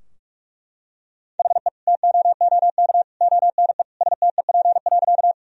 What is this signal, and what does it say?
Morse code, heard as a keyed tone, about 41 wpm, 700 Hz. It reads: HE TOOK ONE STEP.
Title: Sentences from Top 1000 Words - Review Part 2 - 45wpm